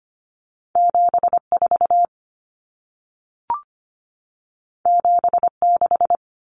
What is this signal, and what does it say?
Morse code, heard as a keyed tone, about 25 wpm, 700 Hz. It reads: 74 76
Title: Common Cut Numbers - 2-Digit 25wpm